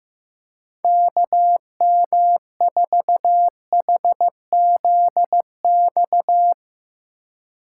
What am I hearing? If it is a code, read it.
Morse code, heard as a keyed tone, about 15 wpm, 700 Hz. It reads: KM4HZX
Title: USA Call Signs - 6 Characters - 3x Character Spacing - 15wpm